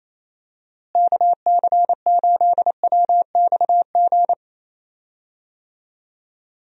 Morse code, heard as a keyed tone, about 28 wpm, 700 Hz. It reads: KC8WXG
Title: USA Call Signs - 6 Characters - Repeated 2x - 28wpm